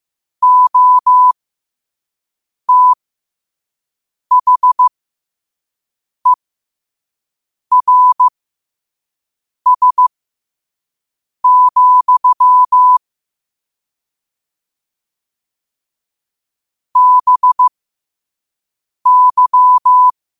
Morse code, heard as a keyed tone, about 15 wpm, spaced as for 5 wpm, 1000 Hz. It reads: OTHERS, BY
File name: qotdfiles / 5wpm